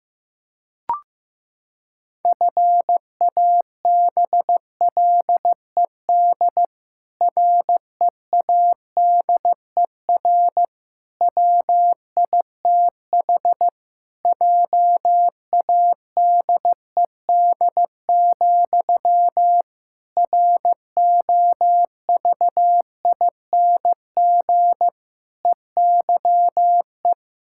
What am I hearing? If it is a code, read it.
Morse code, heard as a keyed tone, about 15 wpm, 700 Hz. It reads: FABLED READER WITH JADED, ROVING EYE
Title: Pangram Challenge v2 - 15wpm